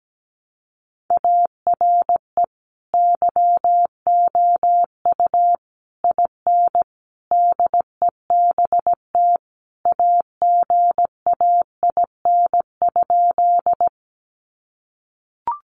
Morse code, heard as a keyed tone, about 17 wpm, 700 Hz. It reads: ARE YOU IN DEBT AGAIN?
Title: Sentences from Top 1300 Words 17wpm